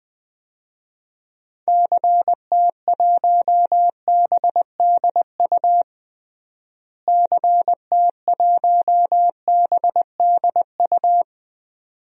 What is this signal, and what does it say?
Morse code, heard as a keyed tone, about 20 wpm, 700 Hz. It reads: CT1BDU CT1BDU
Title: DXCC-100 Call Signs - 6 Characters - Repeated 2x - 20wpm